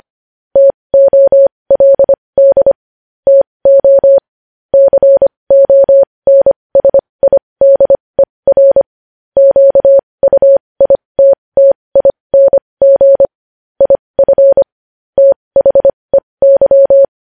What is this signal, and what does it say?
Morse code, heard as a keyed tone, about 25 wpm, spaced as for 20 wpm, 550 Hz. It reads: TOLD TO CONSIDER QUITTING IF THEY
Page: News Headlines in Morse Code at 20 WPM